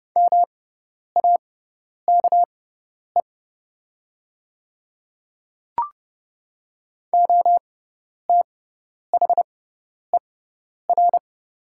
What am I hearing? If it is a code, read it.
Morse code, heard as a keyed tone, about 30 wpm, spaced as for 10 wpm, 700 Hz. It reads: MAKE OTHER
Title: 6x Character Spacing - Top 100 Words - 30wpm